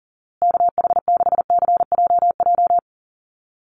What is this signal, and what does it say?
Morse code, heard as a keyed tone, about 40 wpm, 700 Hz. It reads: KH6CJJ
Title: Call Signs Repeated 2x - 40wpm